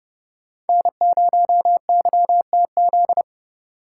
Morse code, heard as a keyed tone, about 29 wpm, 700 Hz. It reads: N0YTZ